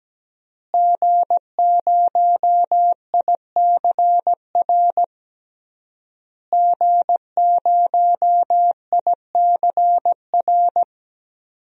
Morse code, heard as a keyed tone, about 17 wpm, 700 Hz. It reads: G0ICR G0ICR
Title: DXCC-100 Call Signs - 5 Characters - Repeated 3x - 17wpm